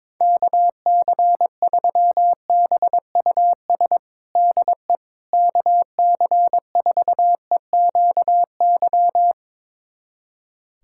Morse code, heard as a keyed tone, about 22 wpm, 700 Hz. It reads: KC3BUS DE KC4EQY
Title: USA Callsign DE Callsign 22wpm